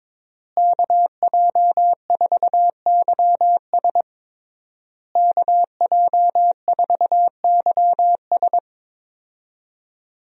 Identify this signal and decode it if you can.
Morse code, heard as a keyed tone, about 22 wpm, 700 Hz. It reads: KJ4YS KJ4YS
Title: USA Call Signs - 5 Characters - Repeated 3x - 22wpm